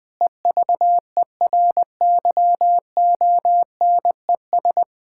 Morse code, heard as a keyed tone, about 20 wpm, 700 Hz. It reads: EVERYONES
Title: Difficult Sentences with Hints - 20wpm